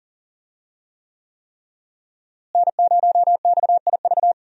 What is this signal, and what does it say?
Morse code, heard as a keyed tone, about 40 wpm, 700 Hz. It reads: N0XIV
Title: USA Call Signs - 5 Characters - Repeated 3x - 40wpm